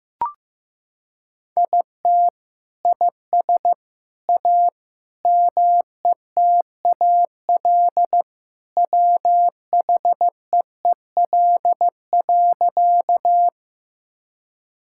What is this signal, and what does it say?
Morse code, heard as a keyed tone, about 15 wpm, 700 Hz. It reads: IT IS A METAL WHEEL.
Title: Sentences from Top 1000 Words - Review Part 6 - 15wpm